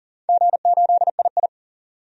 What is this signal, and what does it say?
Morse code, heard as a keyed tone, about 41 wpm, 700 Hz. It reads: G8II
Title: DXCC-100 Call Signs - 4 Characters - 40wpm